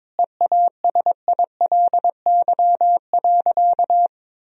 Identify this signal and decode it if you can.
Morse code, heard as a keyed tone, about 22 wpm, 700 Hz. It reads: EASILY.
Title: Sentences from Top 1000 Words - Review Part 3 - 22wpm